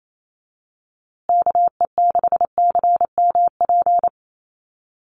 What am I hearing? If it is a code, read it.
Morse code, heard as a keyed tone, about 28 wpm, 700 Hz. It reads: KE6CMP
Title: USA Call Signs Repeated 3x 28wpm